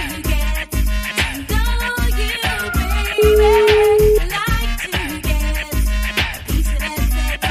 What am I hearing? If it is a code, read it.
Ringback tone (Europe, 425 Hz), with noise.